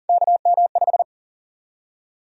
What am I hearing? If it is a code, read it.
Morse code, heard as a keyed tone, about 40 wpm, 700 Hz. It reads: KM5
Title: USA Call Sign Prefixes - Repeated 2x - 40wpm